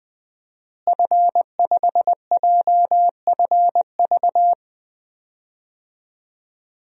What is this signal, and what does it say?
Morse code, heard as a keyed tone, about 21 wpm, 700 Hz. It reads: F5JFV
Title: DXCC-100 Call Signs - 5 Characters - Repeated 2x - 20wpm